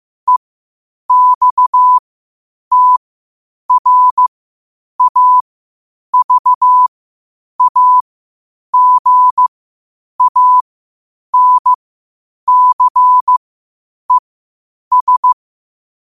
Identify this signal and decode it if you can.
Morse code, heard as a keyed tone, about 15 wpm, spaced as for 8 wpm, 1000 Hz. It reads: EXTRAVAGANCES
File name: qotdfiles / 8wpm